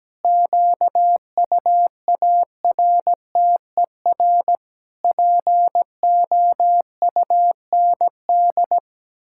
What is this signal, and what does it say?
Morse code, heard as a keyed tone, about 17 wpm, 700 Hz. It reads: QUARTER POUND
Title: Pangram Challenge v2 - 17wpm